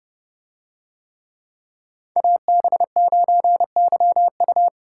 Morse code, heard as a keyed tone, about 29 wpm, 700 Hz. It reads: AB9YU